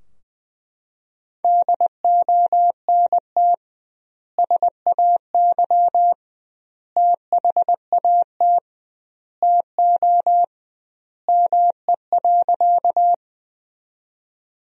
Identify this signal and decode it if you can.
Morse code, heard as a keyed tone, about 21 wpm, 700 Hz. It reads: DONT SAY THAT TO ME.
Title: 2x Word-Spacing - Sentences from Top 300 Words - 20wpm